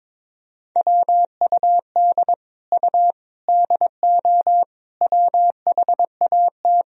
Morse code, heard as a keyed tone, about 22 wpm, 700 Hz. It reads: WUD U DO WHAT
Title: Sentences from Top 100 Words and Abbreviations 22wpm